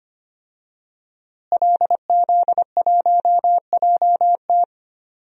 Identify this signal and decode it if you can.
Morse code, heard as a keyed tone, about 25 wpm, 700 Hz. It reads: LZ1JT
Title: DXCC-100 Call Signs - 5 Characters - Repeated 3x - 25wpm